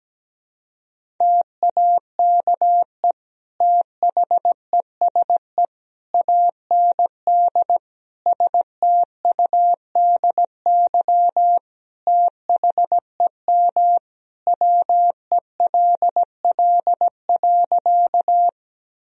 Morse code, heard as a keyed tone, about 17 wpm, 700 Hz. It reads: TAKE THESE AND STUDY THEM WELL.
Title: Sentences from Top 1000 Words - Review Part 1 - 17wpm